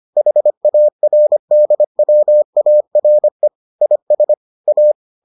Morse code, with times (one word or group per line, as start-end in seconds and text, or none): 0.2-3.5 HARDWARE
3.8-4.3 IS
4.7-4.9 A